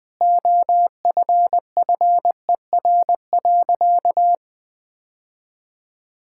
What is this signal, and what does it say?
Morse code, heard as a keyed tone, about 20 wpm, 700 Hz. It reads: OFFER.
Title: Sentences from Top 1600 Words 20wpm